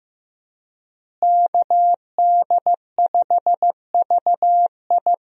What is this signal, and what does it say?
Morse code, heard as a keyed tone, about 16 wpm, 700 Hz. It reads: KD5VI
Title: USA Call Signs - 5 Characters - Repeated 3x - 15wpm